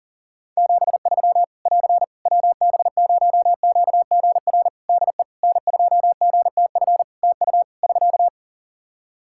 Morse code, heard as a keyed tone, about 40 wpm, 700 Hz. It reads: 73 <AR> WB0QGR DE N2GTF TU <SK>
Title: Practice QSO - Exchange 4 - Ending - 40wpm